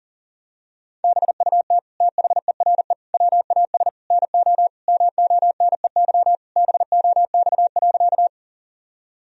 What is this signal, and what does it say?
Morse code, heard as a keyed tone, about 40 wpm, 700 Hz. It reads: BUT THERE WAS NO MONEY BOX.
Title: Sentences from Top 1000 Words - Review Part 2 - 40wpm